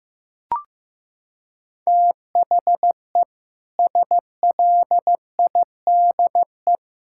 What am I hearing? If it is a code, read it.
Morse code, heard as a keyed tone, about 15 wpm, 700 Hz. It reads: THE SLIDE
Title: Sentences from Top 1600 Words 15wpm